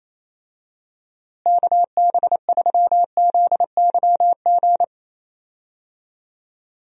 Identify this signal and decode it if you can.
Morse code, heard as a keyed tone, about 28 wpm, 700 Hz. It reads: KB3ZYG
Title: USA Call Signs - 6 Characters - Repeated 3x - 28wpm